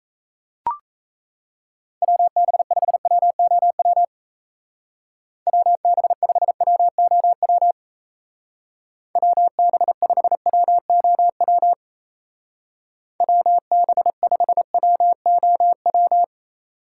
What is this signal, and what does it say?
Morse code, heard as a keyed tone, about 35 wpm, 700 Hz. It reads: WB5WOW WB5WOW WB5WOW WB5WOW